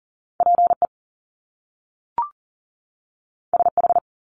Morse code, heard as a keyed tone, about 41 wpm, 700 Hz. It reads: PE SH
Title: Top Two Letter Combinations in Words 40wpm